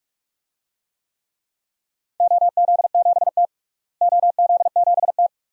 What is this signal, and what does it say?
Morse code, heard as a keyed tone, about 41 wpm, 700 Hz. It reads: OZ7T OZ7T
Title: DXCC-100 Call Signs - 4 Characters - Repeated 2x - 45wpm